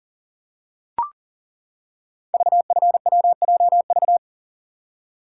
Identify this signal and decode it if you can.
Morse code, heard as a keyed tone, about 39 wpm, 700 Hz. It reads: VFWJV
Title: Code Groups - 5 Letters - 40wpm